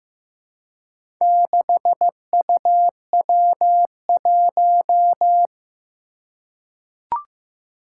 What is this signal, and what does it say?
Morse code, heard as a keyed tone, about 15 wpm, 700 Hz. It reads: 6UW1
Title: Four Challenging Letter-Number Combinations 15wpm